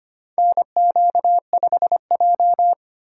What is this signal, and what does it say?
Morse code, heard as a keyed tone, about 25 wpm, 700 Hz. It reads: NQ5J
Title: USA Call Signs - 2x Character Spacing - 25wpm